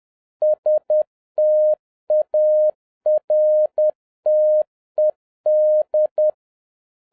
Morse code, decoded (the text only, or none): STARTED